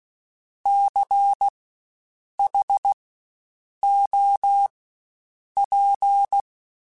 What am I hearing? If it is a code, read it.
Morse code, heard as a keyed tone, about 16 wpm, spaced as for 7 wpm, 800 Hz. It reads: CHOP
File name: Spaced Alphas / Spaced Alphas mp3s / Cs sp @15